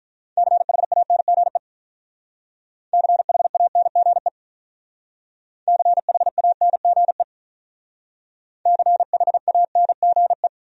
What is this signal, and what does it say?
Morse code, heard as a keyed tone, about 42 wpm, 700 Hz. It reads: CHANGE CHANGE CHANGE CHANGE
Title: Speed-Racing Top 500 Words 35wpm